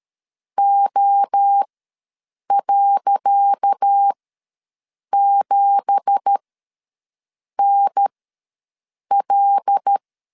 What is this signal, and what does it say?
Morse code, heard as a keyed tone, about 13 wpm, spaced as for 6 wpm, 800 Hz. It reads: O.7NL
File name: LEZIONI / 10 SIGNS / 10C DOT ws 12wpm 3548